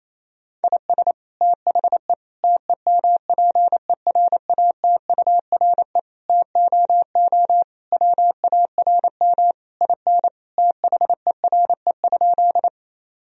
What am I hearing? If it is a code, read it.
Morse code, heard as a keyed tone, about 28 wpm, 700 Hz. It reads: IS THE TEMPERATURE TOO WARM IN THERE?
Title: Sentences from Top 1000 Words - Review Part 4 - 28wpm